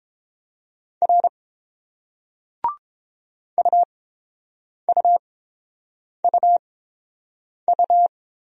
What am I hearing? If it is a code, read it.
Morse code, heard as a keyed tone, about 30 wpm, spaced as for 7 wpm, 690 Hz. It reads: R UUUU